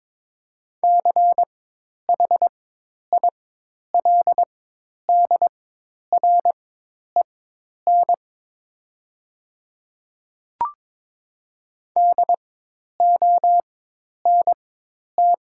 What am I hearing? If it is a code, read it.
Morse code, heard as a keyed tone, about 22 wpm, spaced as for 10 wpm, 700 Hz. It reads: CHILDREN DONT
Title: 4x Character Spacing - Top 300 Words - 22wpm